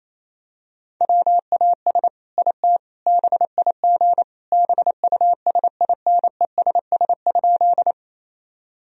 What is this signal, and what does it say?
Morse code, heard as a keyed tone, about 28 wpm, 700 Hz. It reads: WAS IT BIG BUSINESS?
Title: Sentences from Top 1000 Words - Review Part 1 - 28wpm